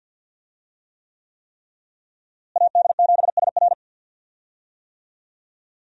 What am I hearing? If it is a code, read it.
Morse code, heard as a keyed tone, about 52 wpm, 700 Hz. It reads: AD7SR
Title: USA Call Signs - 5 Characters - Repeated 3x - 50wpm